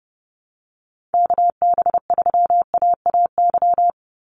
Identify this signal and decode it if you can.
Morse code, heard as a keyed tone, about 30 wpm, 700 Hz. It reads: KB3AAY
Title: Call Signs - 30wpm